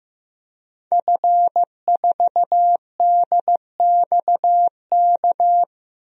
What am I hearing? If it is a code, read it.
Morse code, heard as a keyed tone, about 15 wpm, 700 Hz. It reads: F4DXK
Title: DXCC-100 Call Signs - 5 Characters - Repeated 2x - 15wpm